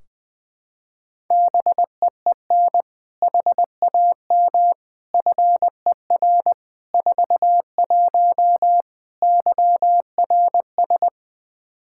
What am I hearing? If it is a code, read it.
Morse code, heard as a keyed tone, about 20 wpm, 700 Hz. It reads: BEEN HAM FER 41 YRS
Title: Practice QSO - Exchange 3 - More Info - 20wpm